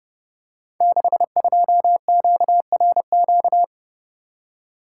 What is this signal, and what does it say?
Morse code, heard as a keyed tone, about 28 wpm, 700 Hz. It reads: 62QRQ
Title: Code Groups - 5 Letter:Numbers - 30wpm